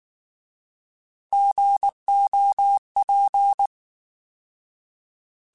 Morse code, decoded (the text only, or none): GOP